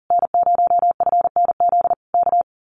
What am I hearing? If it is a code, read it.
Morse code, heard as a keyed tone, about 40 wpm, 700 Hz. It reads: N0FNZ K
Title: Common QSO Phrases Encore 40wpm